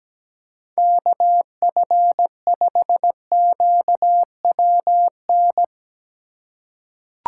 Morse code, heard as a keyed tone, about 17 wpm, 700 Hz. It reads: KF5QWN
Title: USA Call Signs - 6 Characters - 4x Character Spacing - 17wpm